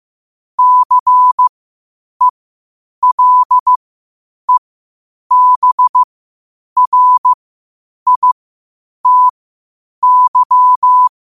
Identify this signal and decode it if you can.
Morse code, heard as a keyed tone, about 15 wpm, spaced as for 8 wpm, 1000 Hz. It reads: CELEBRITY
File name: qotdfiles / 8wpm